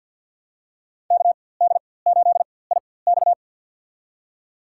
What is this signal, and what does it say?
Morse code, heard as a keyed tone, about 52 wpm, spaced as for 24 wpm, 700 Hz. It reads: KD8IX
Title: USA Call Signs - 5 Characters - 4x Character Spacing - 50wpm